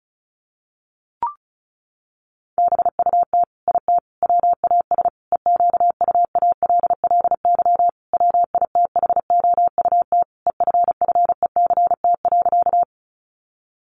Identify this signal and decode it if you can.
Morse code, heard as a keyed tone, about 34 wpm, 700 Hz. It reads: BUT IT WAS EQUALLY WITHOUT EFFECT.
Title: Sentences from Top 1400 Words 35wpm